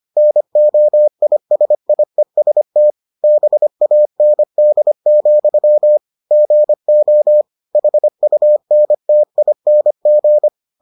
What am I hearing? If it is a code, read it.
Morse code, heard as a keyed tone, about 25 wpm, 600 Hz. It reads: NOISIEST BAND, GO HUNTING